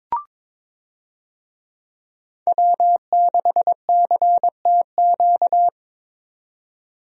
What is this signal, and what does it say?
Morse code, heard as a keyed tone, about 21 wpm, 700 Hz. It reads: W6CTQ
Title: USA Call Signs - 5 Characters - 22wpm